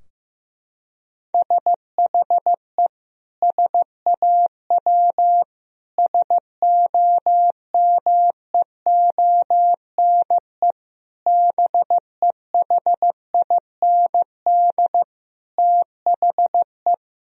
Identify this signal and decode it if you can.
Morse code, heard as a keyed tone, about 15 wpm, 700 Hz. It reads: SHE SAW SOMEONE BEHIND THE
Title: Sentences from Top 1000 Words - Review Part 4 - 15wpm